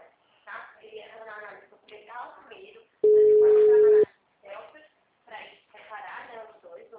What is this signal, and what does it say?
Ringback tone (Europe, 425 Hz), with speech.
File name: out- -2008- - - .54728